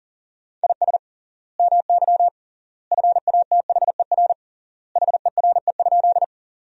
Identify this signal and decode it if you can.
Morse code, heard as a keyed tone, about 37 wpm, 700 Hz. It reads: IS MY FATHER HERE?
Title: 3x Word-Spacing - Sentences from Top 300 Words - 40wpm